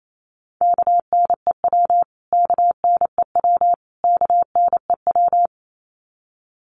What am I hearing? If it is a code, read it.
Morse code, heard as a keyed tone, about 28 wpm, 700 Hz. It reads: KNEW KNEW KNEW
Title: Top 500 Words Repeated 28wpm